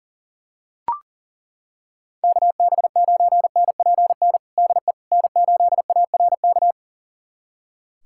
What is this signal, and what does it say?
Morse code, heard as a keyed tone, about 40 wpm, 700 Hz. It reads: KB9NPN DE N8ARK
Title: USA Callsign DE Callsign 40wpm